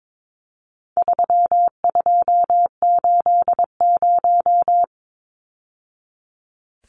Morse code, heard as a keyed tone, about 22 wpm, 700 Hz. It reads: 3280
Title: Numbers - 4-Digit 22wpm